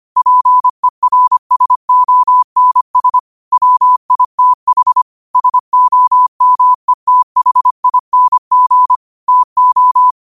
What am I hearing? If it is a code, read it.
Morse code, heard as a keyed tone, about 25 wpm, 1000 Hz. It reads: PERSONS WITH SOMETHING TO